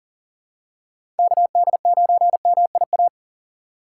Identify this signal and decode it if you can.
Morse code, heard as a keyed tone, about 38 wpm, 700 Hz. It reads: KD9MIA
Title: USA Call Signs - 6 Characters - 4x Character Spacing - 40wpm